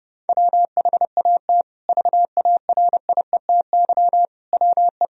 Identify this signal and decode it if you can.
Morse code, heard as a keyed tone, about 32 wpm, 700 Hz. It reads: WHAT VARIETY WE
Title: Sentences from Top 1000 Words - Review Part 4 - 30wpm